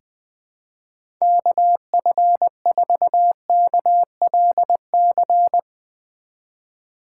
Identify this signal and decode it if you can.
Morse code, heard as a keyed tone, about 20 wpm, 700 Hz. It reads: KF4KLC
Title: USA Call Signs - 6 Characters - 4x Character Spacing - 20wpm